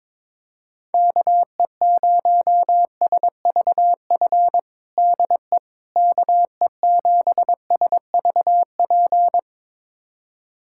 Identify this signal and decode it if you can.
Morse code, heard as a keyed tone, about 22 wpm, 700 Hz. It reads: KE0SVF DE KE7SVP
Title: USA Callsign DE Callsign - 6 Characters - 22wpm